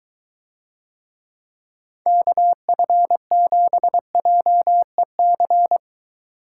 Morse code, heard as a keyed tone, about 23 wpm, 700 Hz. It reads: KF7JEC